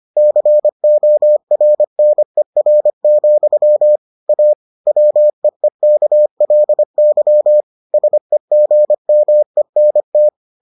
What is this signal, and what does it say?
Morse code, heard as a keyed tone, about 25 wpm, 600 Hz. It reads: CORNER, A WEEKLY SEGMENT